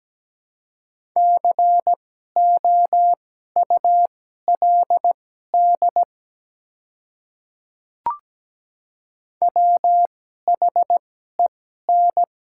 Morse code, heard as a keyed tone, about 17 wpm, spaced as for 12 wpm, 700 Hz. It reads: COULD WHEN
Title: 2x Character Spacing - Top 200 Words - 17wpm